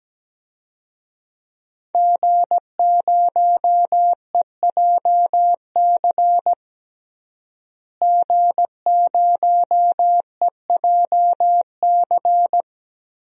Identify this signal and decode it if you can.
Morse code, heard as a keyed tone, about 17 wpm, 700 Hz. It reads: G0EJC G0EJC
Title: DXCC-100 Call Signs - 5 Characters - Repeated 3x - 17wpm